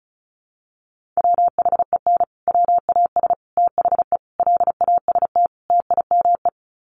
Morse code, heard as a keyed tone, about 34 wpm, 700 Hz. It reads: WHEN WAS THE LAST TIME
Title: Sets of 5 Words Encore 35wpm